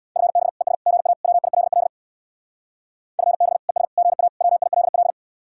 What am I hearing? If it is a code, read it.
Morse code, heard as a keyed tone, about 66 wpm, 700 Hz. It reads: FB ES GA WILL FB ES GA WILL